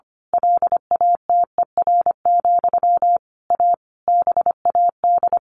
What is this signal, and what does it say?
Morse code, heard as a keyed tone, about 25 wpm, 700 Hz. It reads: LATER, A BAD
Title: A Sound Of Thunder 25wpm